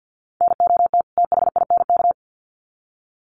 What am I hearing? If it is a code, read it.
Morse code, heard as a keyed tone, about 50 wpm, 700 Hz. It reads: NOT THINK
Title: Sets of 2 Words 50wpm